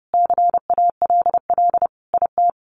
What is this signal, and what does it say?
Morse code, heard as a keyed tone, about 30 wpm, 700 Hz. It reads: CALL IT
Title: Sets of 5 Words Encore2 30wpm